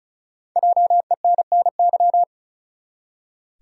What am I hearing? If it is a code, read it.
Morse code, heard as a keyed tone, about 34 wpm, 700 Hz. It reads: JENNY
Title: USA Common Names - 35wpm